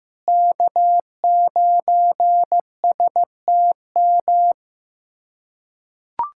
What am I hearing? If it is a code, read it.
Morse code, heard as a keyed tone, about 15 wpm, 700 Hz. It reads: K9STM
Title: K1USN SST QSO Practice - 15wpm